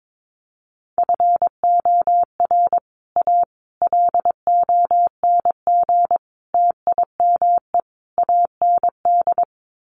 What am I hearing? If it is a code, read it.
Morse code, heard as a keyed tone, about 22 wpm, 700 Hz. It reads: FOR A LONG TIME AND
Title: Sets of 5 Words Encore2 22wpm